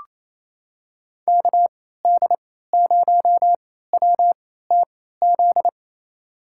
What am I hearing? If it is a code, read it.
Morse code, heard as a keyed tone, about 28 wpm, spaced as for 16 wpm, 700 Hz. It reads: KD0WTZ
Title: USA Call Signs - 6 Characters - 3x Character Spacing - 28wpm